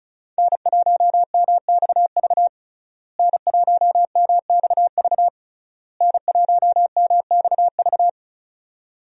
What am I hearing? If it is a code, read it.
Morse code, heard as a keyed tone, about 36 wpm, 700 Hz. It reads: N1MXV N1MXV N1MXV
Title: USA Call Signs - 5 Characters - Repeated 3x - 35wpm